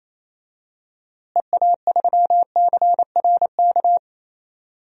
Morse code, heard as a keyed tone, about 27 wpm, 700 Hz. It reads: EA3CRK
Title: DXCC-100 Call Signs - 6 Characters - Repeated 2x - 28wpm